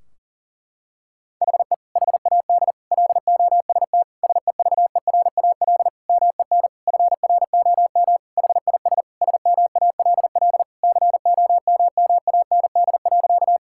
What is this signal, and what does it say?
Morse code, heard as a keyed tone, about 40 wpm, 700 Hz. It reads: HE HAD LOST SEVERAL MEN FROM HIS SMALL COMMAND.
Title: Sentences from Top 1600 Words 40wpm